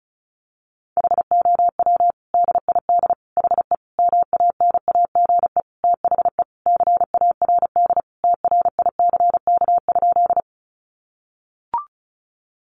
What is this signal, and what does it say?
Morse code, heard as a keyed tone, about 34 wpm, 700 Hz. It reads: HOW DID HE MANAGE THE CARD TRICK?
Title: Sentences from Top 1500 Words 35wpm